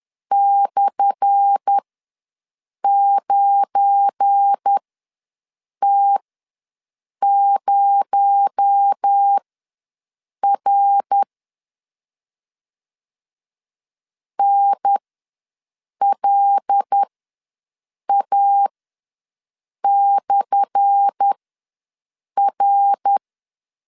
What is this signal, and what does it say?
Morse code, heard as a keyed tone, about 10 wpm, spaced as for 6 wpm, 800 Hz. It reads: /9T0R NLA/R
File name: LEZIONI / 10 SIGNS / 10A SLASH ws 10wpm 4121